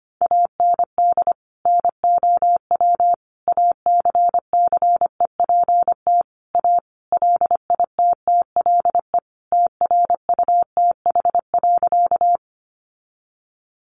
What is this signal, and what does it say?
Morse code, heard as a keyed tone, about 25 wpm, 700 Hz. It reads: AND NOW ACCEPT A LITTLE TRUTH.
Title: Sentences from Top 1300 Words 25wpm